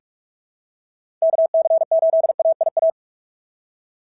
Morse code, heard as a keyed tone, about 46 wpm, 690 Hz. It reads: KC8AIA
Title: SOTA QSO Practice 45wpm